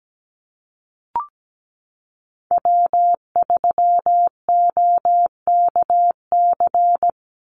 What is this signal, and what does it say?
Morse code, heard as a keyed tone, about 17 wpm, 700 Hz. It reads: W3OKC
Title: Call Signs Repeated 2x - 17wpm